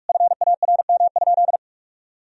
Morse code, heard as a keyed tone, about 46 wpm, 700 Hz. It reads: FARM?
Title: Sentences from Top 1000 Words - Review Part 3 - 45wpm